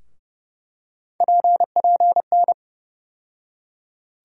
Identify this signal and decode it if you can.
Morse code, heard as a keyed tone, about 32 wpm, 700 Hz. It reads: PPN
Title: USA Call Sign Postfixes - 3 Letters 30wpm